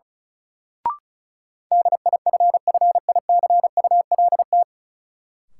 Morse code, heard as a keyed tone, about 36 wpm, 700 Hz. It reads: DIFFICULT
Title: Sets of 4 Words - New Word Drill - 35wpm